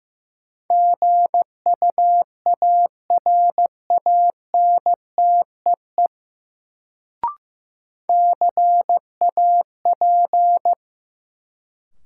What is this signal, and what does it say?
Morse code, heard as a keyed tone, about 15 wpm, 700 Hz. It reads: GUARANTEE CAP
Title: Sentences from Top 1600 Words - New Word Drill Alt - 15wpm